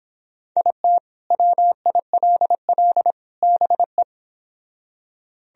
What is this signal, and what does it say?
Morse code, heard as a keyed tone, about 26 wpm, 700 Hz. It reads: IT WILL BE